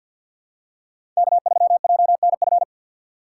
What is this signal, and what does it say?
Morse code, heard as a keyed tone, about 51 wpm, 700 Hz. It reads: K3JNF
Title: USA Call Signs - 5 Characters - 3x Character Spacing - 50wpm